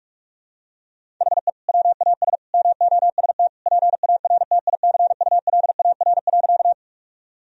Morse code, heard as a keyed tone, about 41 wpm, 700 Hz. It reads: HE WAS MOST PARTICULAR.
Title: Sentences from Top 1000 Words - Review Part 4 - 45wpm